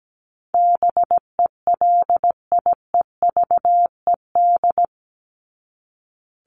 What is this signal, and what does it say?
Morse code, heard as a keyed tone, about 17 wpm, 700 Hz. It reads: BELIEVED
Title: ED Suffix - Top 100 Words with Base Words 17wpm